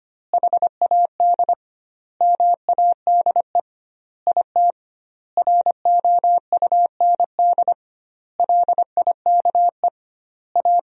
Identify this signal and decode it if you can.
Morse code, heard as a keyed tone, about 25 wpm, 700 Hz. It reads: HAD MADE IT ROUND LIKE A
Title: 2x Word-Spacing - Sentences from Top 500 Words - 25wpm